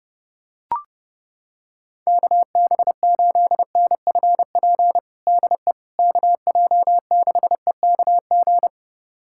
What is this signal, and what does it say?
Morse code, heard as a keyed tone, about 30 wpm, 700 Hz. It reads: KB8NFP DE KJ6EKG
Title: USA Callsign DE Callsign - 6 Characters - 30wpm